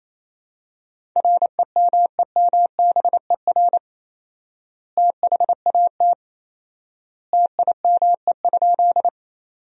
Morse code, heard as a keyed tone, about 28 wpm, 700 Hz. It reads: REMEMBER THAT TIME?
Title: 4x Word-Spacing - Sentences from Top 400 Words - 28wpm